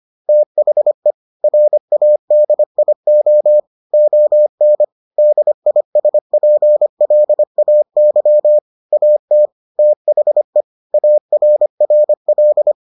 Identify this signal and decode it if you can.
Morse code, heard as a keyed tone, about 25 wpm, 600 Hz. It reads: THE RADIO ON DISPLAY AT THE ARRL